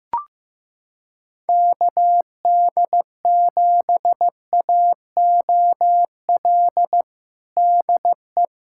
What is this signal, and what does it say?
Morse code, heard as a keyed tone, about 16 wpm, 700 Hz. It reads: KD7AOL DE
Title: USA Callsign DE Callsign 15wpm